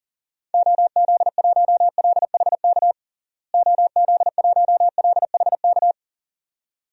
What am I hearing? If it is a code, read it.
Morse code, heard as a keyed tone, about 40 wpm, 700 Hz. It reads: OZ1LHK OZ1LHK
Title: DXCC-100 Call Signs - 6 Characters - Repeated 2x - 40wpm